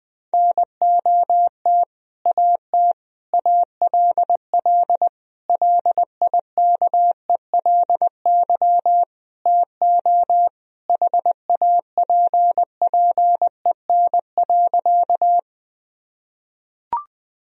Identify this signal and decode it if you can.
Morse code, heard as a keyed tone, about 21 wpm, 700 Hz. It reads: NOT AT ALL LIKELY TO HAPPEN.
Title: Sentences from Top 1000 Words - Review Part 4 - 20wpm